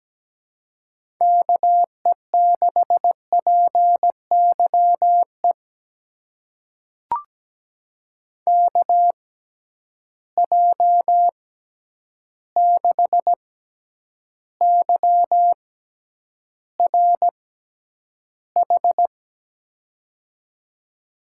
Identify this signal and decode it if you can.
Morse code, heard as a keyed tone, about 17 wpm, spaced as for 9 wpm, 700 Hz. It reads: KE6PYE KJ6YRH
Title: USA Call Signs - 6 Characters - 6x Character Spacing - 17wpm